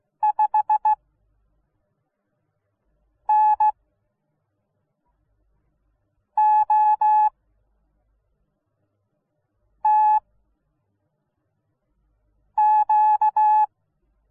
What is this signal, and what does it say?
Morse code, heard as a keyed tone, about 16 wpm, spaced as for 3 wpm, 850 Hz. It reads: 5NOTQ